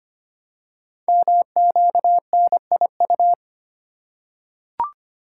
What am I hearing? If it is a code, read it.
Morse code, heard as a keyed tone, about 25 wpm, 700 Hz. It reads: MQNIU
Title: Code Groups - 5 Letters - 25wpm